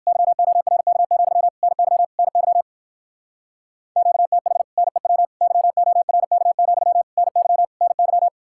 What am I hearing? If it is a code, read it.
Morse code, heard as a keyed tone, about 59 wpm, 700 Hz. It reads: YORK, NY NY QTH NEW YORK, NY NY